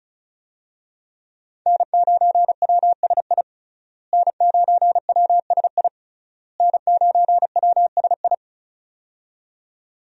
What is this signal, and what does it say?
Morse code, heard as a keyed tone, about 34 wpm, 700 Hz. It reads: N9WSI N9WSI N9WSI
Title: USA Call Signs - 5 Characters - Repeated 3x - 35wpm